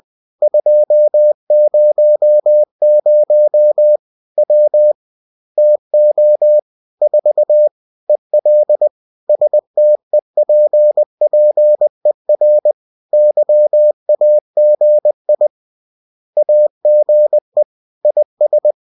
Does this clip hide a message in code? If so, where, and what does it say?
Yes: Morse code from 0.4 to 18.7 s, heard as a keyed tone, about 20 wpm, 600 Hz. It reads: 200 W TO 4 EL STEPPER YAGI AGE IS